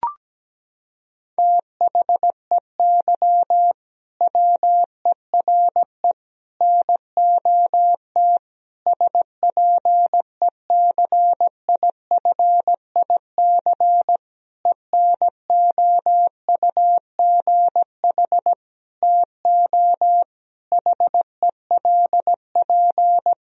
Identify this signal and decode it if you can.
Morse code, heard as a keyed tone, about 17 wpm, 700 Hz. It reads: THEY WERE NOT SPECIFIC ENOUGH TO HELP
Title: Sentences from Top 1000 Words - Review Part 3 - 17wpm